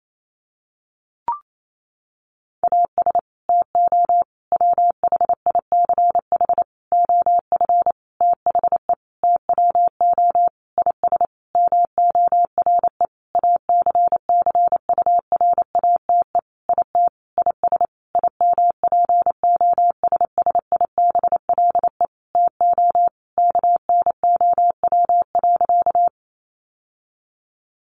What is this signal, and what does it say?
Morse code, heard as a keyed tone, about 28 wpm, 700 Hz. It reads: AS TO WHICH OF THE TWO IS MORE ACCURATE IT IS IMPOSSIBLE TO KNOW.
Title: Sentences from Top 1200 Words 28wpm